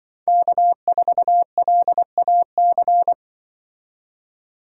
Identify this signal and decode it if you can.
Morse code, heard as a keyed tone, about 24 wpm, 700 Hz. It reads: K4LAC
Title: Speed Racing - USA Call Signs 20wpm